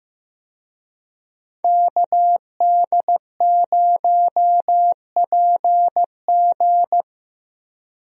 Morse code, heard as a keyed tone, about 15 wpm, 700 Hz. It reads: KD0PG
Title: USA Call Signs - 5 Characters - Repeated 3x - 15wpm